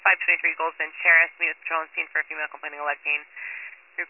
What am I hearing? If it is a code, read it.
Emergency Alert System (SAME) tones, heard as harsh buzzing bursts, with speech. It is heard from 0.9 to 3.8 s.